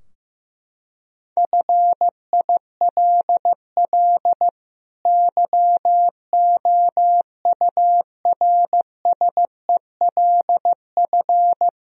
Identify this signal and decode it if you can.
Morse code, heard as a keyed tone, about 15 wpm, 700 Hz. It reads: FILL YOURSELF
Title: Sentences from Top 1000 Words - Review Part 3 - 15wpm